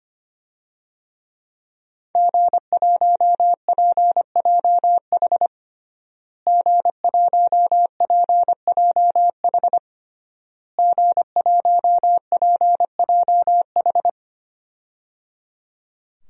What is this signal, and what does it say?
Morse code, heard as a keyed tone, about 25 wpm, 700 Hz. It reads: G1PJH G1PJH G1PJH
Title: DXCC-100 Call Signs - 5 Characters - Repeated 3x - 25wpm